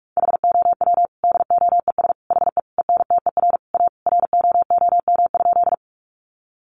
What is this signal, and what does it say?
Morse code, heard as a keyed tone, about 46 wpm, 700 Hz. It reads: HOW DOES HE ENTER A ROOM?
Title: Sentences from Top 1300 Words 45wpm